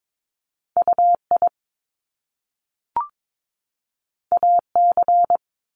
Morse code, heard as a keyed tone, about 22 wpm, 700 Hz. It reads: UI AC
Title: Top Two Letter Combinations in Words 22wpm